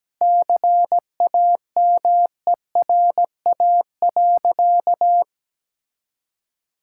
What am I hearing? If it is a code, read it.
Morse code, heard as a keyed tone, about 17 wpm, 700 Hz. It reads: CAMERA.
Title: Sentences from Top 1000 Words - Review Part 5 - 17wpm